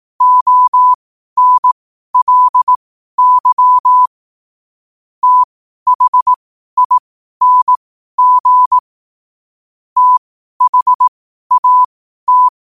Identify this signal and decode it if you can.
Morse code, heard as a keyed tone, about 18 wpm, spaced as for 12 wpm, 1000 Hz. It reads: ONLY THING THAT
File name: qotdfiles / 12wpm